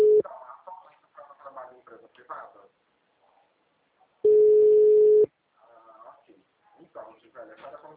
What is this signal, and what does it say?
Ringback tone (Europe, 425 Hz), with speech.
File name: out- -2034- - - .44565